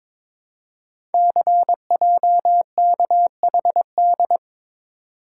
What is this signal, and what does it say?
Morse code, heard as a keyed tone, about 22 wpm, 700 Hz. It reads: CJKHD
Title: Code Groups - 5 Letter:Numbers - 22wpm